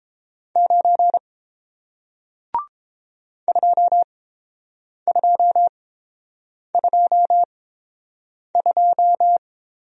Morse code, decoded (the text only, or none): 9 2222